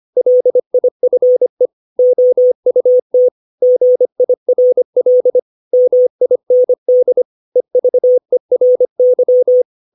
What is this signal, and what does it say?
Morse code, heard as a keyed tone, about 25 wpm, 500 Hz. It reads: LIFE OUT GIRL MIND EVERY
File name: 632 words / 632 Words 25WPM